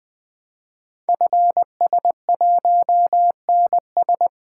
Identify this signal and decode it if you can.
Morse code, heard as a keyed tone, about 20 wpm, 700 Hz. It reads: FS1NS
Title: Code Groups - 5 Letter:Numbers - 20wpm